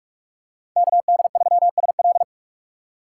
Morse code, heard as a keyed tone, about 46 wpm, 700 Hz. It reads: KD3SL
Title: USA Call Signs - 5 Characters - 2x Character Spacing - 45wpm